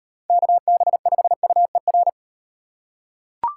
Morse code, heard as a keyed tone, about 38 wpm, 700 Hz. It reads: KB5UER